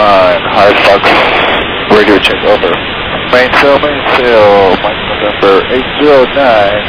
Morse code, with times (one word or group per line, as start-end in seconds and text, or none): none